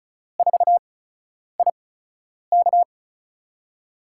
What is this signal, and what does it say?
Morse code, heard as a keyed tone, about 36 wpm, spaced as for 10 wpm, 700 Hz. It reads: 4IK
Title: USA Call Signs - 8x Character Spacing - 35wpm